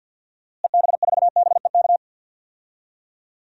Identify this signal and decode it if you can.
Morse code, heard as a keyed tone, about 51 wpm, 700 Hz. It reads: EB4BEK
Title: DXCC-100 Call Signs - 6 Characters - Repeated 3x - 50wpm